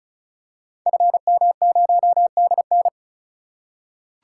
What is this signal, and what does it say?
Morse code, heard as a keyed tone, about 36 wpm, 700 Hz. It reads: FM0DN
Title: Code Groups - 5 Letter:Numbers - 35wpm